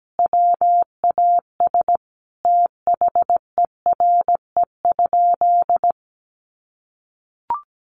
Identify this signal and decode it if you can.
Morse code, heard as a keyed tone, about 17 wpm, 700 Hz. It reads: WAS THERE?
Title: Sentences from Top 1200 Words 17wpm